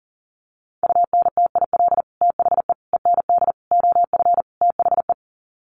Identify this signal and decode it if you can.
Morse code, heard as a keyed tone, about 40 wpm, 700 Hz. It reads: UNTIL THE END OF THE
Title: Sets of 5 Words Encore 40wpm